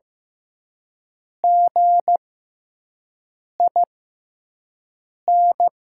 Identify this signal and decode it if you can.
Morse code, heard as a keyed tone, about 15 wpm, spaced as for 5 wpm, 700 Hz. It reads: GIN
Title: 6x Character Spacing - Top 500 Words - 15wpm